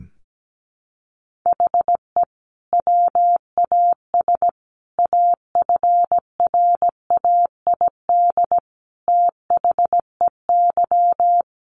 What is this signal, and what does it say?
Morse code, heard as a keyed tone, about 17 wpm, 700 Hz. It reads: HE WAS AFRAID THEY
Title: Sentences from Top 1500 Words 17wpm